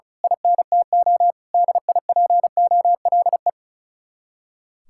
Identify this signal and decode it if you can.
Morse code, heard as a keyed tone, about 33 wpm, 700 Hz. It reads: INTO DIPOLE
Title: Speed-Racing Common QSO Phrases - 35wpm